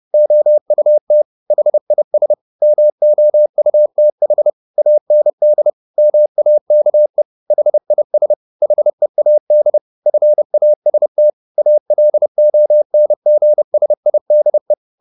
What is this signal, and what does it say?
Morse code, heard as a keyed tone, about 30 wpm, 600 Hz. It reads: OUT HIS MOUTH AND MAKE HIS HEAD FAST ALONGSIDE